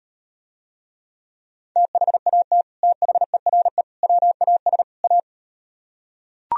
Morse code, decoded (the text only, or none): THAT THERE WAS A